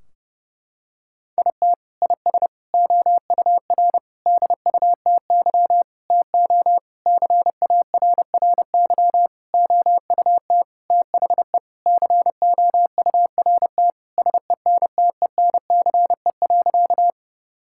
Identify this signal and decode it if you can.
Morse code, heard as a keyed tone, about 32 wpm, 700 Hz. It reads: IT IS OUR DUTY TO CARRY OUT THE COURT SENTENCE.
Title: Sentences from Top 1600 Words 30wpm